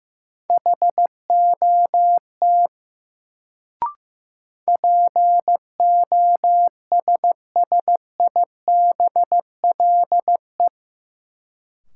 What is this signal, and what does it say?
Morse code, heard as a keyed tone, about 15 wpm, 700 Hz. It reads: HOT POSSIBLE
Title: Sentences from Top 500 Words - New Word Drill - 15wpm